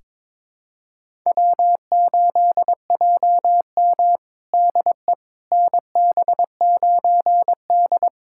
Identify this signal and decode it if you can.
Morse code, heard as a keyed tone, about 22 wpm, 700 Hz. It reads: W8JM DE NB9D
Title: USA Callsign DE Callsign - 4 Characters - 22wpm